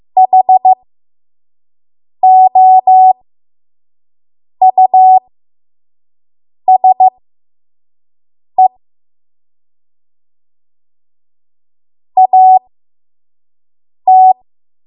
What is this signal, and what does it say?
Morse code, heard as a keyed tone, about 15 wpm, spaced as for 5 wpm, 750 Hz. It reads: HOUSE AT
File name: Morse / 05WPM